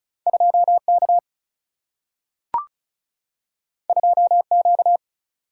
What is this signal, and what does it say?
Morse code, heard as a keyed tone, about 34 wpm, 700 Hz. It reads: 2K 2Q
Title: Single-Letter Single-Number 35wpm